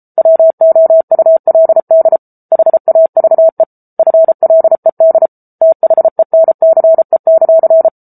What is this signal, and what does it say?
Morse code, heard as a keyed tone, about 34 wpm, 650 Hz. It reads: WOULD HAVE FLED THENCE;
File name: Book Two / 07-Chapter 7 - 33 WPM